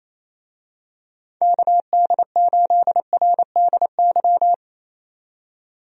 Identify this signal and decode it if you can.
Morse code, heard as a keyed tone, about 27 wpm, 700 Hz. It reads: KD8RDY
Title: USA Call Signs - 6 Characters - 28wpm